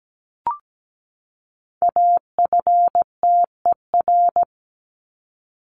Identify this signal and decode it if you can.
Morse code, heard as a keyed tone, about 17 wpm, 700 Hz. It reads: AFTER
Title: Top 200 Words 17wpm